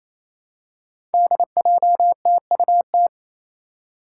Morse code, heard as a keyed tone, about 28 wpm, 700 Hz. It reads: DJTUT
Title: Five Challenging Letter-Number Combinations 28wpm